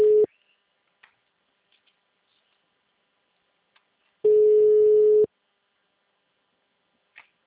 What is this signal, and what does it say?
Ringback tone (Europe, 425 Hz).